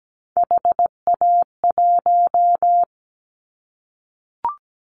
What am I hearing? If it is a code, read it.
Morse code, heard as a keyed tone, about 17 wpm, 700 Hz. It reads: HA1
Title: Callsign Prefixes for Top 25 DXCC - Repeated 3x - 17wpm